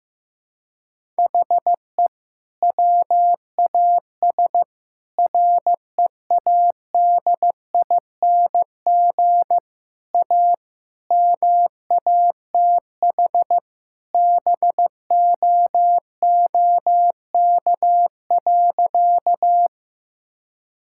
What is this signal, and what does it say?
Morse code, heard as a keyed tone, about 16 wpm, 700 Hz. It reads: HE WAS READING A MATH BOOK.
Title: Sentences from Top 1000 Words - Review Part 6 - 15wpm